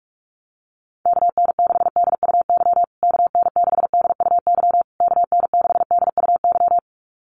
Morse code, heard as a keyed tone, about 44 wpm, 700 Hz. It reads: KN6DUY KN6DUY KN6DUY
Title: USA Call Signs Repeated 3x 45wpm